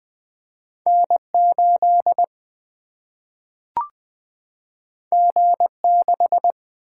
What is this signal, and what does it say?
Morse code, heard as a keyed tone, about 20 wpm, 700 Hz. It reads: N8 G6
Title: Single-Letter Single-Number 20wpm